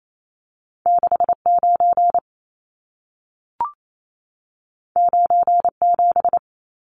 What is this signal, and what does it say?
Morse code, heard as a keyed tone, about 28 wpm, 700 Hz. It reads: 69 97
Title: Numbers - 2-Digit 28wpm